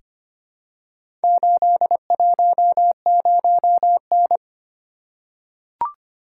Morse code, heard as a keyed tone, about 26 wpm, 700 Hz. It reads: 810N
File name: practice / 25wpm / Four Letter-Number Combinations - 25wpm